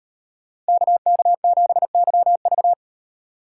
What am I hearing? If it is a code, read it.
Morse code, heard as a keyed tone, about 37 wpm, 700 Hz. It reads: KK7YV